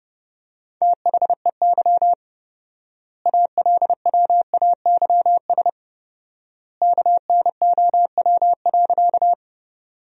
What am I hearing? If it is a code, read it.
Morse code, heard as a keyed tone, about 32 wpm, 700 Hz. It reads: THEY ALWAYS KNOW.
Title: 4x Word-Spacing - Sentences from Top 300 Words - 30wpm